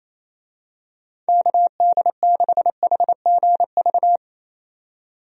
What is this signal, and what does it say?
Morse code, heard as a keyed tone, about 27 wpm, 700 Hz. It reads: KD6HGV
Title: USA Call Signs - 6 Characters - 28wpm